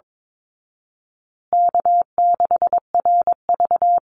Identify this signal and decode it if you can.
Morse code, heard as a keyed tone, about 22 wpm, 700 Hz. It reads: K6RV
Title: Call Signs Repeated 2x - 22wpm